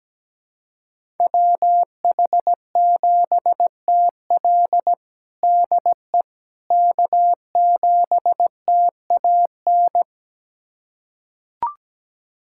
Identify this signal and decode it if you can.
Morse code, heard as a keyed tone, about 17 wpm, 700 Hz. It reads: WH7TL DE K7TAN
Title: USA Callsign DE Callsign - 5 Characters - 17wpm